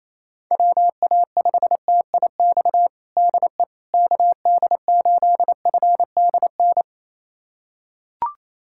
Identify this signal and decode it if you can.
Morse code, heard as a keyed tone, about 28 wpm, 700 Hz. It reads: WA5TIX DE KD8FDN
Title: USA Callsign DE Callsign - 6 Characters - 28wpm